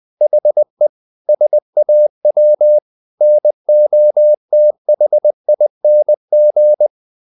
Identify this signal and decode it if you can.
Morse code, heard as a keyed tone, about 20 wpm, 600 Hz. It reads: HE SAW NOTHING
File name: the old man and the sea part 2 20wpm